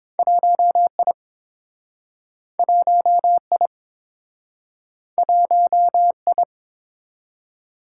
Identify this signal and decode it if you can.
Morse code, heard as a keyed tone, about 29 wpm, 700 Hz. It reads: 1I 1I 1I